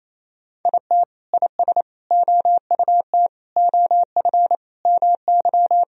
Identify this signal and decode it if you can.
Morse code, heard as a keyed tone, about 28 wpm, 700 Hz. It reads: IT IS OUT OF MY
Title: Sentences from Top 1000 Words - Review Part 2 - 28wpm